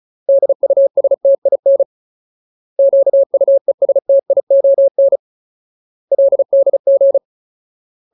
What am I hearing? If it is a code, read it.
Morse code, heard as a keyed tone, about 34 wpm, 550 Hz. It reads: DUSTIN QUESTION LDG
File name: Ultimate Word List / 35WPM Ultimate